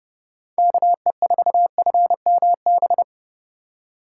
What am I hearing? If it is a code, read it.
Morse code, heard as a keyed tone, about 31 wpm, 700 Hz. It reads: KE4FMB